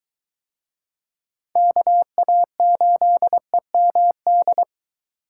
Morse code, heard as a keyed tone, about 23 wpm, 700 Hz. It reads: KA8EMD